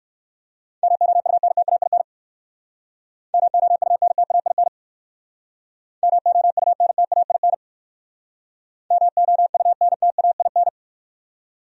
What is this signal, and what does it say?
Morse code, heard as a keyed tone, about 55 wpm, 700 Hz. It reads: MOUNTAIN MOUNTAIN MOUNTAIN MOUNTAIN
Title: Speed-Racing Top 300 Words 45wpm